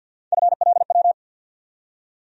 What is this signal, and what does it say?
Morse code, heard as a keyed tone, about 47 wpm, 700 Hz. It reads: FLW
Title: USA Call Sign Postfixes - 3 Letters 50wpm